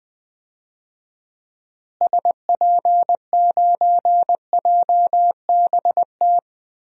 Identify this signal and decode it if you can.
Morse code, heard as a keyed tone, about 20 wpm, 700 Hz. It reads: SP9JBT
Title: DXCC-100 Call Signs - 6 Characters - Repeated 3x - 20wpm